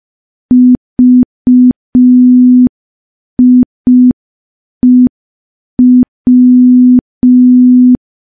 Morse code, heard as a keyed tone, about 5 wpm, 250 Hz. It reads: VIEW
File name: morse / morse-training-05WPM